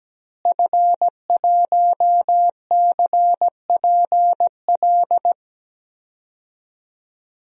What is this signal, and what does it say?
Morse code, heard as a keyed tone, about 17 wpm, 700 Hz. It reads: F1CPL
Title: DXCC-100 Call Signs - 5 Characters - Repeated 2x - 17wpm